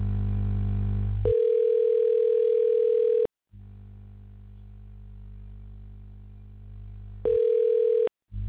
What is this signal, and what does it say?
Ringback tone (North America), 440 + 480 Hz.